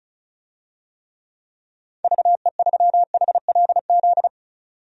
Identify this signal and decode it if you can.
Morse code, heard as a keyed tone, about 34 wpm, 700 Hz. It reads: VE3HLZ
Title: DXCC-100 Call Signs - 6 Characters - 35wpm